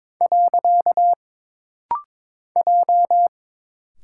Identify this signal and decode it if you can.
Morse code, heard as a keyed tone, about 22 wpm, 700 Hz. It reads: . J
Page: This rapid-fire timing is geared to push your ICR (Instant Character Recognition) skills.